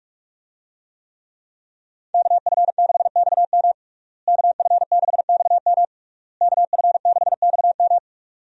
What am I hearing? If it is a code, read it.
Morse code, heard as a keyed tone, about 46 wpm, 700 Hz. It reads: KF6XM KF6XM KF6XM
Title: USA Call Signs - 5 Characters - Repeated 3x - 45wpm